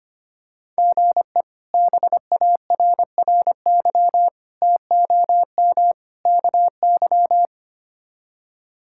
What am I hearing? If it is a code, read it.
Morse code, heard as a keyed tone, about 25 wpm, 700 Hz. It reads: GE BARRY TOM KY
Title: K1USN SST QSO Practice - 25wpm